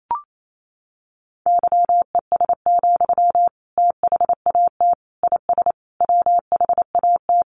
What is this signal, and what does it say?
Morse code, heard as a keyed tone, about 28 wpm, 700 Hz. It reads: YES, THAT IS WHAT
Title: Sentences from Top 500 Words 28wpm